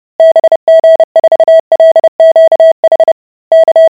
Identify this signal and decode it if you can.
Morse code, heard as a keyed tone, about 30 wpm, 650 Hz. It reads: BG4LQH K